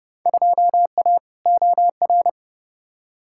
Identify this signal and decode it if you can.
Morse code, heard as a keyed tone, about 30 wpm, 700 Hz. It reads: 2A OR
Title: Field Day Exchanges Encore - 30wpm